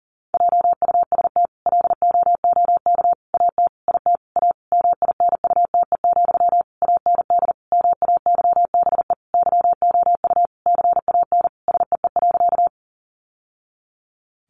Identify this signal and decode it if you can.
Morse code, heard as a keyed tone, about 40 wpm, 700 Hz. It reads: JUST LOOK AT IT A MINUTE, AND MAYBE YOU CAN SEE.
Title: Sentences from Top 1000 Words 40wpm